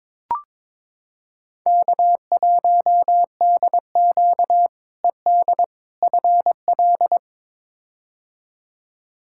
Morse code, heard as a keyed tone, about 22 wpm, 700 Hz. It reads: K1DQ ED FL
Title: K1USN SST QSO Practice - 22wpm